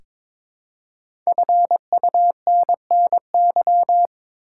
Morse code, heard as a keyed tone, about 22 wpm, 700 Hz. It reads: FUNNY
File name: 5 Letter Words - First 3 Letters Given - 22wpm